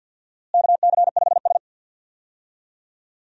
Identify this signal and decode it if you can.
Morse code, heard as a keyed tone, about 52 wpm, 700 Hz. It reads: KC5S
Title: USA Call Signs - 4 Characters - Repeated 2x - 50wpm